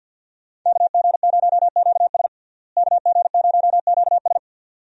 Morse code, heard as a keyed tone, about 49 wpm, 700 Hz. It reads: KG0QS KG0QS
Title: USA Call Signs - 5 Characters - Repeated 2x - 50wpm